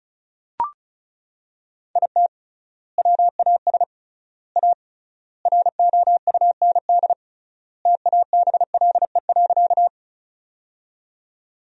Morse code, heard as a keyed tone, about 36 wpm, 700 Hz. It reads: IT WAS A ROUND TABLE.
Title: 3x Word-Spacing - Sentences from Top 400 Words - 35wpm